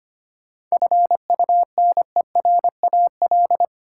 Morse code, heard as a keyed tone, about 25 wpm, 700 Hz. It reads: FUNERAL
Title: Seven Letter Words - 25wpm